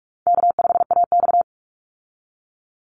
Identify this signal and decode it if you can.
Morse code, heard as a keyed tone, about 46 wpm, 700 Hz. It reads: K5AX
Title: Easy Call Signs Repeated 3x - 45wpm